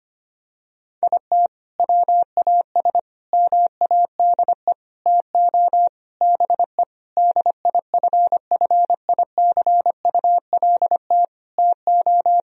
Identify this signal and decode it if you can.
Morse code, heard as a keyed tone, about 25 wpm, 700 Hz. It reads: IT WAS MADE TO BE DIFFICULT TO
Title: Sentences from Top 1000 Words - Review Part 4 - 25wpm